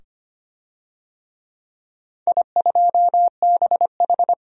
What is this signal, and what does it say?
Morse code, heard as a keyed tone, about 25 wpm, 700 Hz. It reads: I2BH
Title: DXCC-100 Call Signs - 4 Characters - 25wpm